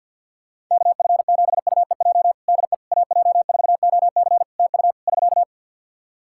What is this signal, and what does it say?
Morse code, heard as a keyed tone, about 51 wpm, 700 Hz. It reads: KF7UEJ DE AJ4OC TU <SK>
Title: Practice QSO - Exchange 4 - Ending - 50wpm